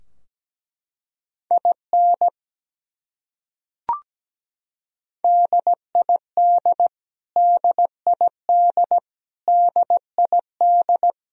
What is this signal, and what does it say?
Morse code, heard as a keyed tone, about 17 wpm, 700 Hz. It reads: IN DID DID DID
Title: Top 100 Words Repeated 17wpm